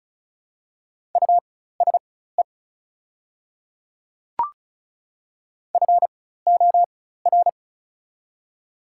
Morse code, heard as a keyed tone, about 34 wpm, spaced as for 16 wpm, 700 Hz. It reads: USE FOR
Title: 4x Character Spacing - Top 100 Words - 35wpm